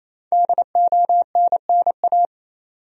Morse code, heard as a keyed tone, about 28 wpm, 700 Hz. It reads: DONNA
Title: USA Common Names - 28wpm